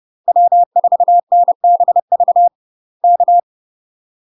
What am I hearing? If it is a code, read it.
Morse code, heard as a keyed tone, about 30 wpm, 700 Hz. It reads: W4NBV K
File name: UK MASTER QSOs TRAINING / 30WPM / UK MASTER QSOs 0051